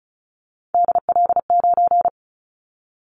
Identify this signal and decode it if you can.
Morse code, heard as a keyed tone, about 34 wpm, 700 Hz. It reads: DL9
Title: Callsign Prefixes for Top 25 DXCC - Repeated 3x - 35wpm